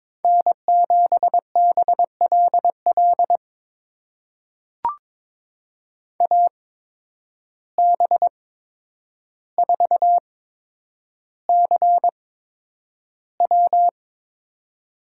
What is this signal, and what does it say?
Morse code, heard as a keyed tone, about 22 wpm, spaced as for 9 wpm, 700 Hz. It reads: N7BLL AB4CW
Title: USA Call Signs - 5 Characters - 8x Character Spacing - 22wpm